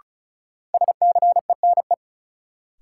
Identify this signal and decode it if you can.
Morse code, heard as a keyed tone, about 34 wpm, 700 Hz. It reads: SCENE
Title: Sentences from Top 1100 Words - New Word Drill - 35wpm